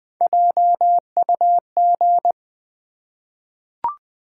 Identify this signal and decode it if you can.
Morse code, heard as a keyed tone, about 20 wpm, 700 Hz. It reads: JUG
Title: Easy Words with Hints - 20wpm